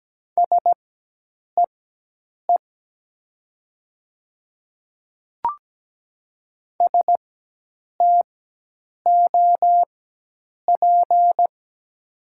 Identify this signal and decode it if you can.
Morse code, heard as a keyed tone, about 17 wpm, spaced as for 8 wpm, 700 Hz. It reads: SEE STOP
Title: 4x Character Spacing - Top 500 Words - 17wpm